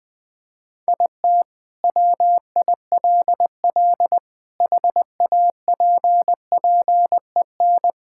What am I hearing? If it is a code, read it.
Morse code, heard as a keyed tone, about 21 wpm, 700 Hz. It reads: IT WILL HAPPEN
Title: Sentences from Top 1000 Words - Review Part 5 - 20wpm